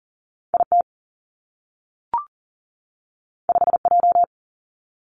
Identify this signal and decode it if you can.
Morse code, heard as a keyed tone, about 42 wpm, 700 Hz. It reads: I T 5 J
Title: Two Challenging Letter-Number Combinations 40wpm